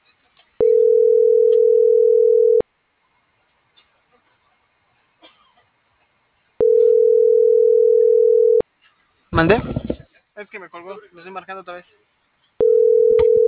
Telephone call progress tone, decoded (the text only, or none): ringback tone